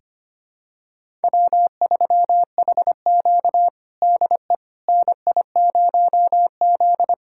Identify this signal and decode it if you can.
Morse code, heard as a keyed tone, about 25 wpm, 700 Hz. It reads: W3HQ DE NI0Z
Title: USA Callsign DE Callsign - 4 Characters - 25wpm